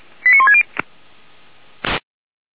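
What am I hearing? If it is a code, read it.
Five-tone selective call (ZVEI), heard as quick runs of five tones, tones of about 70 ms. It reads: 89160
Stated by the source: ZVEI five-tone selective call received by SDR.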